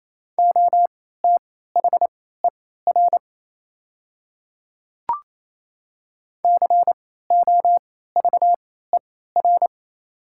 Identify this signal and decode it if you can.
Morse code, heard as a keyed tone, about 28 wpm, spaced as for 16 wpm, 700 Hz. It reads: OTHER COVER
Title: 3x Character Spacing - Top 300 Words - 28wpm